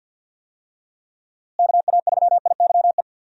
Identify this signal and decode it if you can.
Morse code, heard as a keyed tone, about 51 wpm, 700 Hz. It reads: KA3IYE
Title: USA Call Signs - 6 Characters - Repeated 2x - 50wpm